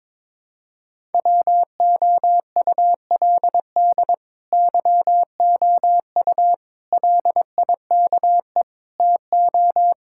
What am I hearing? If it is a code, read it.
Morse code, heard as a keyed tone, about 22 wpm, 700 Hz. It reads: WOULD YOU LIKE TO
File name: practice / 22wpm / Sentences with 2-Word Phrases - 22wpm